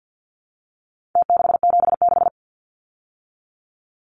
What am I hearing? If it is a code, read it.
Morse code, heard as a keyed tone, about 50 wpm, 700 Hz. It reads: T676
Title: Numbers - Contest Style - 4-Digit 50wpm